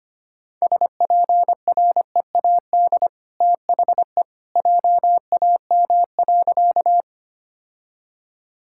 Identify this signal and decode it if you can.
Morse code, heard as a keyed tone, about 25 wpm, 690 Hz. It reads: SPREAD THE JAM.
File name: Easy Sentences with Hints - 25wpm